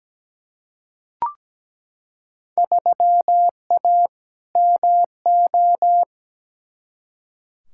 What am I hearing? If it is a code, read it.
Morse code, heard as a keyed tone, about 17 wpm, 700 Hz. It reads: 3A MO
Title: Field Day Exchanges - 17wpm